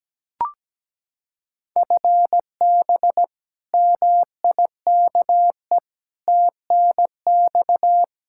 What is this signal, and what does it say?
Morse code, heard as a keyed tone, about 17 wpm, 700 Hz. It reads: FB MIKE TNX
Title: Practice QSO - Exchange 3 - More Info - 17wpm